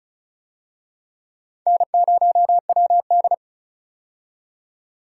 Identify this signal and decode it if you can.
Morse code, heard as a keyed tone, about 36 wpm, 700 Hz. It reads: N0WD
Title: USA Call Signs - 4 Characters - 35wpm